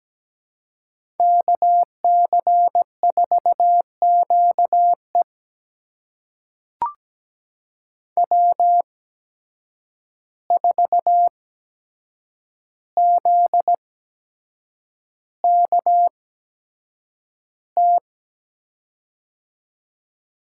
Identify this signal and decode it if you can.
Morse code, heard as a keyed tone, about 17 wpm, spaced as for 7 wpm, 700 Hz. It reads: KC4QE W4ZKT
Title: USA Call Signs - 8x Character Spacing - 17wpm